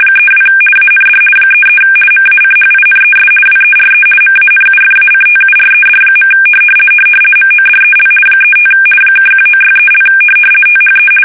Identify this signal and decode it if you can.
RTTY, about 50 baud. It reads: 32 AND 33 MACHINES WERE MANUFACTURED. MODEL 35 (1963) THE
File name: fsk850-50B / teletype-corp